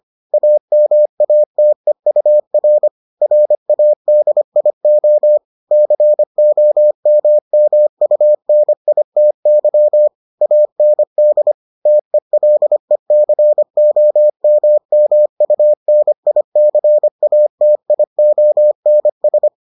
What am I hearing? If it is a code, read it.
Morse code, heard as a keyed tone, about 25 wpm, 600 Hz. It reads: AMATEUR RADIO COMMUNITY AND TELECOMMUNICATIONS